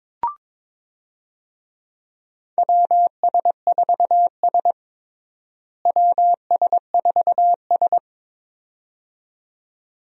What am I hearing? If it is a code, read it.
Morse code, heard as a keyed tone, about 22 wpm, 700 Hz. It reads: WS4S WS4S
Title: USA Call Signs - 4 Characters - Repeated 2x - 22wpm